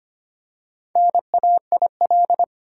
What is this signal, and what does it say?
Morse code, heard as a keyed tone, about 25 wpm, 700 Hz. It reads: NAIL
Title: Four Letter Words - 25wpm